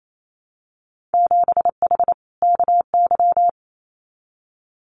Morse code, heard as a keyed tone, about 28 wpm, 700 Hz. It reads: 7H KY
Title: Winter Field Day Exchanges Encore 28wpm